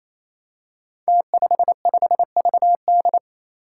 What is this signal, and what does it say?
Morse code, heard as a keyed tone, about 29 wpm, 700 Hz. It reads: T55VD
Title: Five Challenging Letter-Number Combinations 28wpm